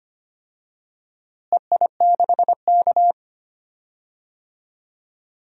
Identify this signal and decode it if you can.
Morse code, heard as a keyed tone, about 25 wpm, 700 Hz. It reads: EI6K
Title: DXCC-100 Call Signs - 4 Characters - Repeated 2x - 25wpm